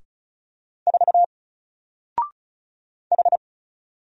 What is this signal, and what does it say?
Morse code, heard as a keyed tone, about 35 wpm, 700 Hz. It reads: 4 H
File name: Single-Letter-Number-35wpm